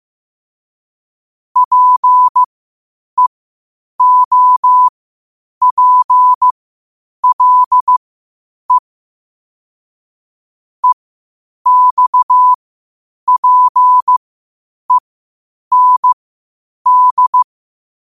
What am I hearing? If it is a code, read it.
Morse code, heard as a keyed tone, about 15 wpm, spaced as for 8 wpm, 1000 Hz. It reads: PEOPLE EXPEND
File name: qotdfiles / 8wpm